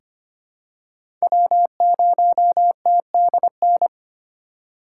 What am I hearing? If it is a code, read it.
Morse code, heard as a keyed tone, about 25 wpm, 700 Hz. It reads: W0TDN
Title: USA Call Signs - 5 Characters - 8x Character Spacing - 25wpm